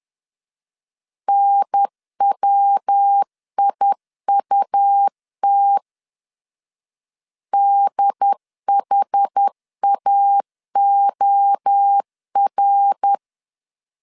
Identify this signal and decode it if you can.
Morse code, heard as a keyed tone, about 10 wpm, 800 Hz. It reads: NWIUT DHAOR
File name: CW / DET / 1 3 1500C S B 10wpm 3335